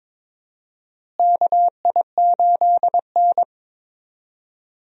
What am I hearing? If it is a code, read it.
Morse code, heard as a keyed tone, about 22 wpm, 700 Hz. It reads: KI8N
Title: USA Call Signs - 4 Characters - Repeated 2x - 22wpm